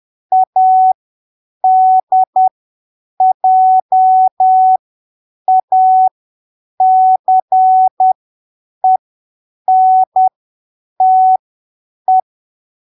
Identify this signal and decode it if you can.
Morse code, heard as a keyed tone, about 10 wpm, spaced as for 7 wpm, 750 Hz. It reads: ADJACENTE
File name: arquivos / texto / texto 10 10ppm